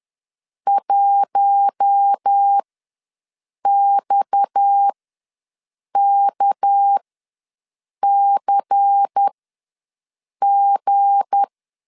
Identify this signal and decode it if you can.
Morse code, heard as a keyed tone, about 10 wpm, spaced as for 6 wpm, 800 Hz. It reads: 1XKCG
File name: LEZIONI / 7 GF9Z / 7 G ws 10wpm 3843